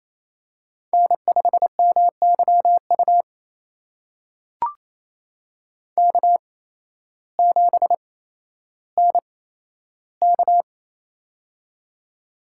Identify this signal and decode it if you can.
Morse code, heard as a keyed tone, about 28 wpm, spaced as for 13 wpm, 700 Hz. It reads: N5MYU K7NK
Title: USA Call Signs - 8x Character Spacing - 28wpm